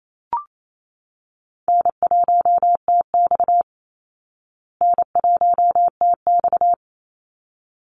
Morse code, heard as a keyed tone, about 28 wpm, 700 Hz. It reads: N1TX N1TX
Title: Call Signs Repeated 3x - 28wpm